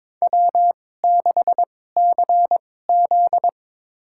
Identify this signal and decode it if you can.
Morse code, heard as a keyed tone, about 22 wpm, spaced as for 16 wpm, 700 Hz. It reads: W6CZ
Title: USA Call Signs - 4 Characters - 2x Character Spacing - 22wpm